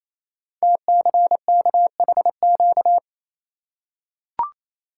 Morse code, heard as a keyed tone, about 28 wpm, 700 Hz. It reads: TCKHQ
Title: Code Groups - 5 Letters - 28wpm